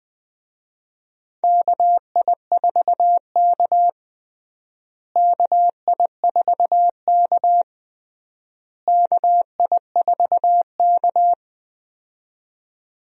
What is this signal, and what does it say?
Morse code, heard as a keyed tone, about 21 wpm, 700 Hz. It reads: KI4K KI4K KI4K
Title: USA Call Signs - 4 Characters - Repeated 3x - 20wpm